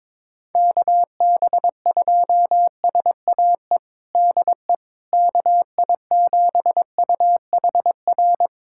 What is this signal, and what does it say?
Morse code, heard as a keyed tone, about 22 wpm, 700 Hz. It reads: KB2SAE DE KI7UHR
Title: USA Callsign DE Callsign - 6 Characters - 22wpm